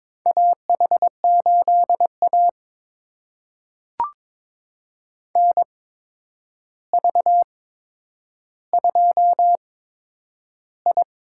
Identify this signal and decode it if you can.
Morse code, heard as a keyed tone, about 22 wpm, spaced as for 9 wpm, 700 Hz. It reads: AH8A NV2I
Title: USA Call Signs - 4 Characters - 8x Character Spacing - 22wpm